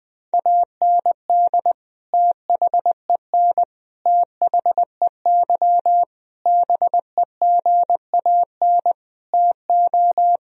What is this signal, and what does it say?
Morse code, heard as a keyed tone, about 20 wpm, 700 Hz. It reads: AND THEN THEY BEGAN TO
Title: Sentences from Top 1000 Words - Review Part 1 - 20wpm